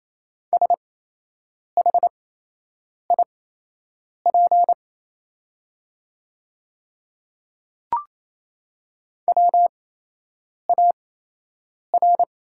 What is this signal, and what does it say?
Morse code, heard as a keyed tone, about 28 wpm, spaced as for 8 wpm, 690 Hz. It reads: SHIP WAR
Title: 8x Character Spacing - Top 500 Words - 28wpm